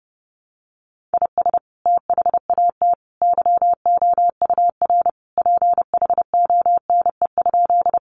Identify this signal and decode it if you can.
Morse code, heard as a keyed tone, about 30 wpm, 700 Hz. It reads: IS THAT YOUR PHONE?
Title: Sentences from Top 1100 Words 30wpm